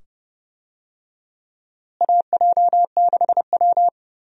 Morse code, heard as a keyed tone, about 29 wpm, 700 Hz. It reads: AJ6W
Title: USA Call Signs - 4 Characters - Repeated 2x - 30wpm